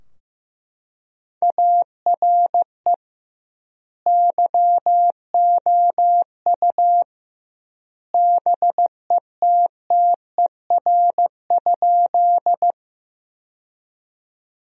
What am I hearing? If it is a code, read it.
Morse code, heard as a keyed tone, about 15 wpm, 700 Hz. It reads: ARE YOU BETTER?
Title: 2x Word-Spacing - Sentences from Top 400 Words - 15wpm